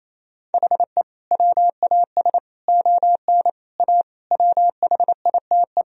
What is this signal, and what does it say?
Morse code, heard as a keyed tone, about 28 wpm, 700 Hz. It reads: HE WAS ON A WHITE
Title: Sentences from Top 1000 Words - Review Part 1 - 28wpm